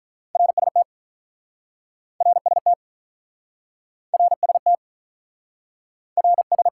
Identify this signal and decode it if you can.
Morse code, heard as a keyed tone, about 47 wpm, 700 Hz. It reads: RST RST RST RS